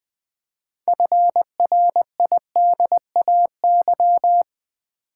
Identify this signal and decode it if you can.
Morse code, heard as a keyed tone, about 20 wpm, 690 Hz. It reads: FRIDAY
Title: Build A Word - 2 Letters At A Time - 6-Letter Words - 20wpm